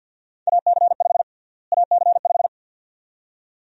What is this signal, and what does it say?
Morse code, heard as a keyed tone, about 53 wpm, 700 Hz. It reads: AC5 AC5
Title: USA Call Sign Prefixes - Repeated 2x - 50wpm